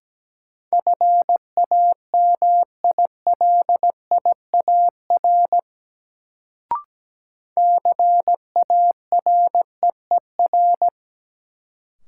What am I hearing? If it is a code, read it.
Morse code, heard as a keyed tone, about 17 wpm, 700 Hz. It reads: FAMILIAR CAREER
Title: Sentences from Top 1000 Words - New Word Drill - 17wpm